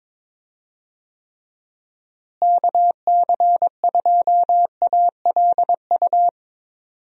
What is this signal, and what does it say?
Morse code, heard as a keyed tone, about 21 wpm, 700 Hz. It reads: KC2ALU
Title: USA Call Signs - 6 Characters - Repeated 2x - 22wpm